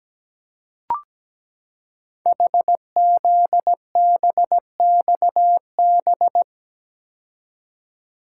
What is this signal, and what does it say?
Morse code, heard as a keyed tone, about 17 wpm, 700 Hz. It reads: HZBXB
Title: Code Groups - 5 Letters - 17wpm